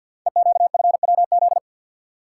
Morse code, heard as a keyed tone, about 48 wpm, 700 Hz. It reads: EQFWZ
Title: Code Groups - 5 Letter:Numbers - 50wpm